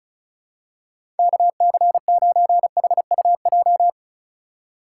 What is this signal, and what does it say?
Morse code, heard as a keyed tone, about 34 wpm, 700 Hz. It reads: KC9HUJ
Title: USA Call Signs - 6 Characters - 3x Character Spacing - 35wpm